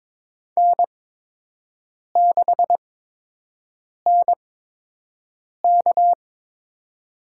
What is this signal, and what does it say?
Morse code, heard as a keyed tone, about 22 wpm, spaced as for 6 wpm, 700 Hz. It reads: N6NK
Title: USA Call Signs - 8x Character Spacing - 22wpm